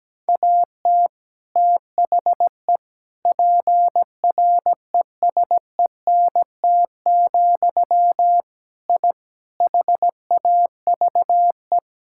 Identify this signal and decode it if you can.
Morse code, heard as a keyed tone, about 17 wpm, 700 Hz. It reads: AT THE PRESENT, I HAVE
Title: Sentences from Top 1000 Words - Review Part 5 - 17wpm